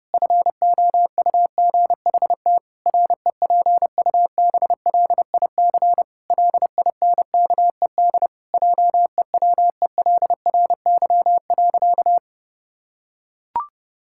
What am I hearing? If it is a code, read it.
Morse code, heard as a keyed tone, about 32 wpm, 700 Hz. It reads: FOUGHT REPUBLIC LINKED JEWELRY.
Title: Pangram Challenge v2 - 30wpm